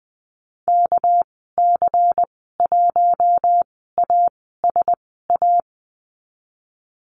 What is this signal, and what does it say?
Morse code, heard as a keyed tone, about 20 wpm, spaced as for 15 wpm, 700 Hz. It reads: KC1ASA
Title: 207 Lesson 33g - Letter j - Call Signs 20wpm